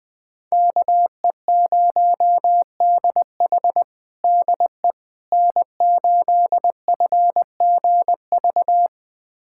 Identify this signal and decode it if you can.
Morse code, heard as a keyed tone, about 20 wpm, 700 Hz. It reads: KE0DH DE N8FGV
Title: USA Callsign DE Callsign - 5 Characters - 20wpm